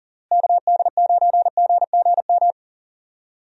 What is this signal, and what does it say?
Morse code, heard as a keyed tone, about 40 wpm, 700 Hz. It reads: KD9GGM
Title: USA Call Signs - 6 Characters - Repeated 3x - 40wpm